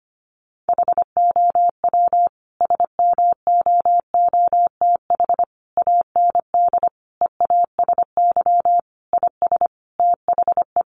Morse code, heard as a keyed tone, about 25 wpm, 700 Hz. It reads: HOW SMOOTH AND EASY IS THE
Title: Sentences from Top 1400 Words 25wpm